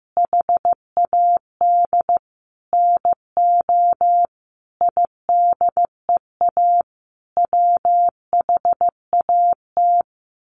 Morse code, heard as a keyed tone, about 15 wpm, 700 Hz. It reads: HAD NO IDEA WHAT
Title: Sets of 5 Words Encore 15wpm